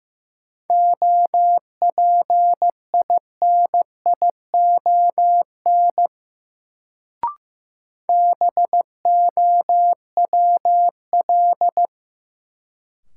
Morse code, heard as a keyed tone, about 15 wpm, 700 Hz. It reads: OPINION BOWL
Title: Sentences from Top 1200 Words - New Word Drill Alt - 15wpm